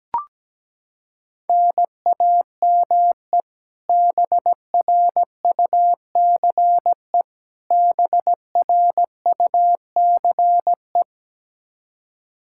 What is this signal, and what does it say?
Morse code, heard as a keyed tone, about 17 wpm, 700 Hz. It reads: NAME BRUCE BRUCE
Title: Practice QSO - Exchange 1 - Beginning - 17wpm